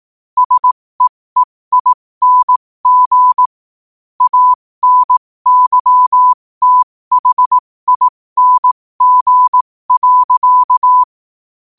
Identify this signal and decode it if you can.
Morse code, heard as a keyed tone, about 18 wpm, spaced as for 15 wpm, 1000 Hz. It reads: SEEING ANYTHING.
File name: qotdfiles / 15wpm